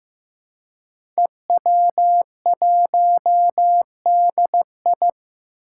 Morse code, heard as a keyed tone, about 15 wpm, 700 Hz. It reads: EW1DI
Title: Five Challenging Letter-Number Combinations 15wpm